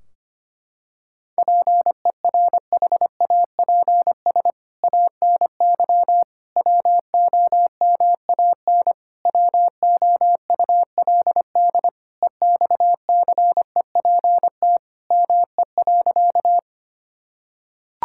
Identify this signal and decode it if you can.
Morse code, heard as a keyed tone, about 25 wpm, 700 Hz. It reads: PERHAPS ANY WOMAN WOULD EXCEPT ME.
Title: Sentences from Top 1600 Words 25wpm